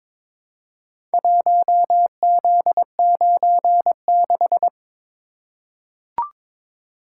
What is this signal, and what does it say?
Morse code, heard as a keyed tone, about 22 wpm, 700 Hz. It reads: 1Z96
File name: practice / 22wpm / Four Letter-Number Combinations - 22wpm